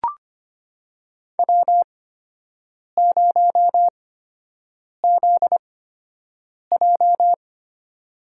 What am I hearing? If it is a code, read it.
Morse code, heard as a keyed tone, about 25 wpm, spaced as for 7 wpm, 700 Hz. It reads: W0ZJ
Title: USA Call Signs - 4 Characters - 8x Character Spacing - 25wpm